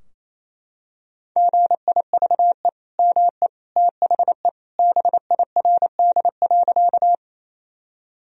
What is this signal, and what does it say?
Morse code, heard as a keyed tone, about 27 wpm, 700 Hz. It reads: GIVE ME THE BIRD.
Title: Sentences from Top 1000 Words - Review Part 2 - 28wpm